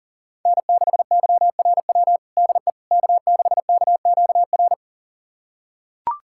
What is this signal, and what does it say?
Morse code, heard as a keyed tone, about 37 wpm, 700 Hz. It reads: N6YRW DE K6KQR
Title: USA Callsign DE Callsign - 5 Characters - 40wpm